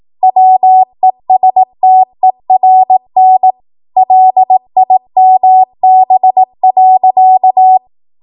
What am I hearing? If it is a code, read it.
Morse code, heard as a keyed tone, about 18 wpm, 750 Hz. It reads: WESTERN LIMB.
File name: Morse / 18WPM